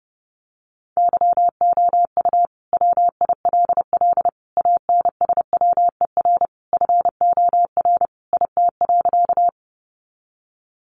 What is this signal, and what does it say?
Morse code, heard as a keyed tone, about 30 wpm, 700 Hz. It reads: YOU WILL ANSWER FOR IT.
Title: Sentences from Top 300 Words 30wpm